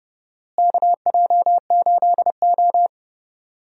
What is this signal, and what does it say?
Morse code, heard as a keyed tone, about 30 wpm, 700 Hz. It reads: KJ8O
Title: USA Call Signs - 4 Characters - Repeated 2x - 30wpm